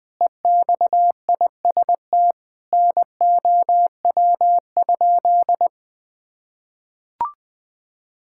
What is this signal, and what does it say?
Morse code, heard as a keyed tone, about 21 wpm, 700 Hz. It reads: EXIST NOW?
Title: Sentences from Top 1000 Words - Review Part 5 - 20wpm